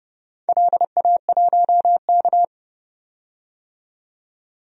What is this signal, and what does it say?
Morse code, heard as a keyed tone, about 28 wpm, 700 Hz. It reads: LA1K
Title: DXCC-100 Call Signs - 4 Characters - 30wpm